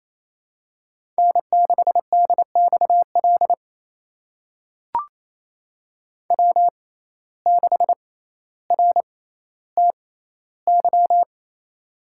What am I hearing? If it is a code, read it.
Morse code, heard as a keyed tone, about 28 wpm, spaced as for 14 wpm, 700 Hz. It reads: N6DXL W6RTY
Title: USA Call Signs - 5 Characters - 6x Character Spacing - 28wpm